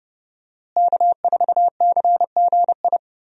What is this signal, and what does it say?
Morse code, heard as a keyed tone, about 30 wpm, 700 Hz. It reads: K4CGI
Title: USA Call Signs - 5 Characters - Repeated 3x - 30wpm